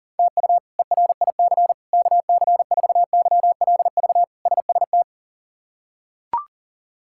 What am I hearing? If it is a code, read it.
Morse code, heard as a keyed tone, about 40 wpm, 700 Hz. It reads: TU ERIC KC4YLV SST
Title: K1USN SST QSO Practice - 40wpm